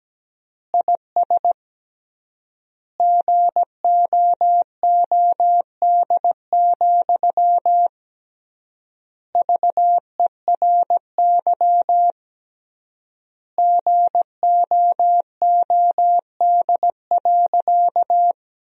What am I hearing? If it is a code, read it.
Morse code, heard as a keyed tone, about 17 wpm, 700 Hz. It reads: IS GOOD, VERY GOOD.
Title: 3x Word-Spacing - Sentences from Top 200 Words - 17wpm